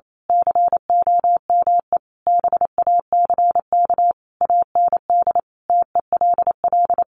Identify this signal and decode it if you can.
Morse code, heard as a keyed tone, about 28 wpm, 700 Hz. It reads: COME BACK AND TELL
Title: Sentences from Top 1100 Words 28wpm